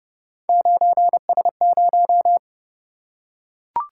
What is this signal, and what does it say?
Morse code, heard as a keyed tone, about 30 wpm, 700 Hz. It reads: 9S0